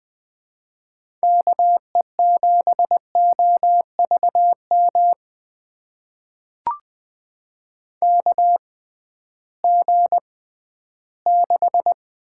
Morse code, heard as a keyed tone, about 20 wpm, spaced as for 13 wpm, 700 Hz. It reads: KE7OVM KG6
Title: USA Call Signs - 6 Characters - 6x Character Spacing - 20wpm